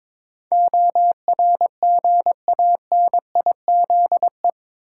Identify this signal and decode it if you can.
Morse code, heard as a keyed tone, about 22 wpm, 700 Hz. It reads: ORGANIZE
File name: practice / 22wpm / Eight Letter Words - 22wpm